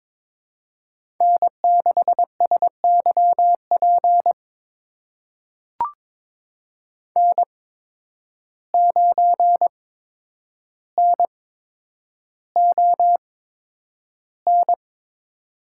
Morse code, heard as a keyed tone, about 22 wpm, spaced as for 10 wpm, 700 Hz. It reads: N6SYP N9NON
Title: USA Call Signs - 5 Characters - 8x Character Spacing - 22wpm